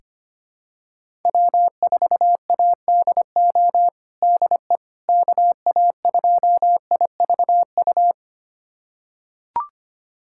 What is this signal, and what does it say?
Morse code, heard as a keyed tone, about 25 wpm, 700 Hz. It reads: W4ADO DE KA2IVU
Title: USA Callsign DE Callsign 25wpm